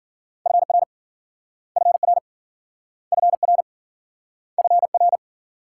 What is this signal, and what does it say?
Morse code, heard as a keyed tone, about 50 wpm, 700 Hz. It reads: FR FR FR FR
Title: Speed-Racing Multi-letter Phonemes - 40wpm